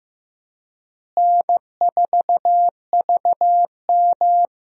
Morse code, heard as a keyed tone, about 15 wpm, 700 Hz. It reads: N4VM
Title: USA Call Signs - 4 Characters - Repeated 3x - 15wpm